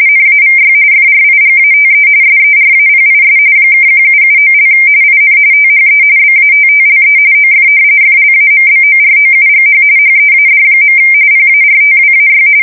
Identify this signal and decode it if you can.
RTTY, about 45.45 baud. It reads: NAVY FRIGATE SERAPIS (1779) IN THE BATTLE OF FLAMBOROUGH HEAD. PARTWAY
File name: fsk170 / usnavy1